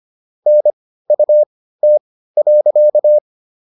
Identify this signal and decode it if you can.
Morse code, heard as a keyed tone, about 25 wpm, spaced as for 15 wpm, 600 Hz. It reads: NUT.
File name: SS119 15